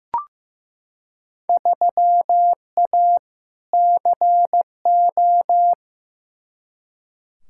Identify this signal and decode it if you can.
Morse code, heard as a keyed tone, about 15 wpm, 700 Hz. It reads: 3A CO
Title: Field Day Exchanges Encore - 15wpm